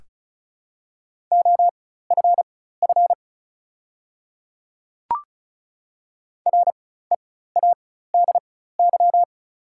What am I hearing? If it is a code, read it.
Morse code, heard as a keyed tone, about 36 wpm, spaced as for 16 wpm, 690 Hz. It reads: OFF READY
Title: 4x Character Spacing - Top 500 Words - 35wpm